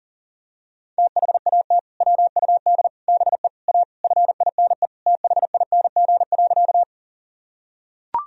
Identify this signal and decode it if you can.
Morse code, heard as a keyed tone, about 40 wpm, 700 Hz. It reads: THAT WUD BE A FINE THING.
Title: Sentences from Top 500 Words and Abbreviations 40wpm